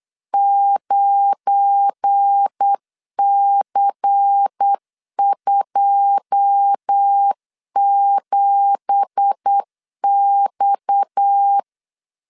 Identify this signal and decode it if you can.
Morse code, heard as a keyed tone, about 8 wpm, 800 Hz. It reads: 9C27X
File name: CW / DET / 1 9 1500C S B 8wpm 5624